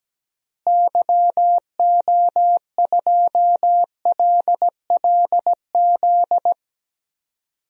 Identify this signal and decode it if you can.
Morse code, heard as a keyed tone, about 17 wpm, 700 Hz. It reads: YO2LLZ
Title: DXCC-100 Call Signs - 6 Characters - Repeated 2x - 17wpm